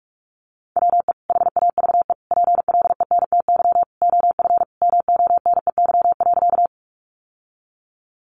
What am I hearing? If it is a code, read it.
Morse code, heard as a keyed tone, about 44 wpm, 700 Hz. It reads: WE HAVE PLENTY OF MONEY.
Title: Sentences from Top 1300 Words 45wpm